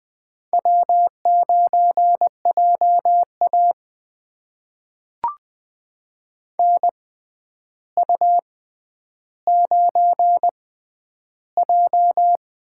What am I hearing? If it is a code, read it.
Morse code, heard as a keyed tone, about 20 wpm, spaced as for 10 wpm, 700 Hz. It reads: W9JA NU9J
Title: USA Call Signs - 4 Characters - 6x Character Spacing - 20wpm